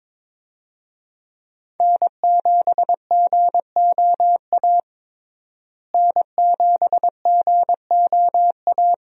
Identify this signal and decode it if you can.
Morse code, heard as a keyed tone, about 22 wpm, 700 Hz. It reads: N7GOA N7GOA
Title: USA Call Signs - 5 Characters - Repeated 2x - 22wpm